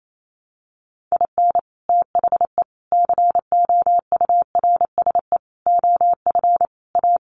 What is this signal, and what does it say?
Morse code, heard as a keyed tone, about 28 wpm, 700 Hz. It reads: IN THE COURSE OF A
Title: Sets of 5 Words Encore2 28wpm